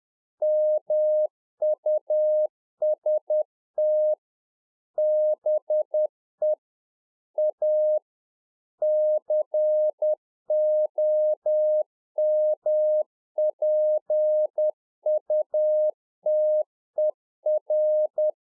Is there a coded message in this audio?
Yes: Morse code from 0.4 to 18.3 s, heard as a keyed tone, about 10 wpm, 610 Hz. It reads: MUST BE A COMPUTER